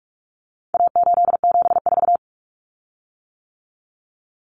Morse code, heard as a keyed tone, about 44 wpm, 700 Hz. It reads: A874
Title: Common Cut Numbers - 4-Digit 45wpm